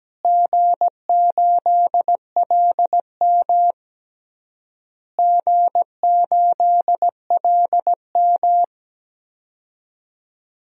Morse code, heard as a keyed tone, about 17 wpm, 700 Hz. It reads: G8LM G8LM
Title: DXCC-100 Call Signs - 4 Characters - Repeated 3x - 17wpm